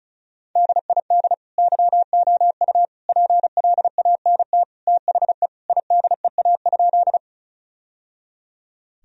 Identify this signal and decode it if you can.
Morse code, heard as a keyed tone, about 34 wpm, 700 Hz. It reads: DID YOU PLANT THE IDEA?
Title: Sentences from Top 1000 Words - Review Part 1 - 35wpm